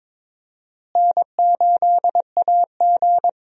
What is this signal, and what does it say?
Morse code, heard as a keyed tone, about 21 wpm, 700 Hz. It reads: N8AG
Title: USA Call Signs - 4 Characters - Repeated 2x - 22wpm